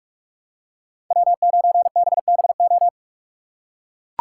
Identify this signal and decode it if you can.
Morse code, heard as a keyed tone, about 41 wpm, 700 Hz. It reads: W9BBO
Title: USA Call Signs - 5 Characters - 2x Character Spacing - 45wpm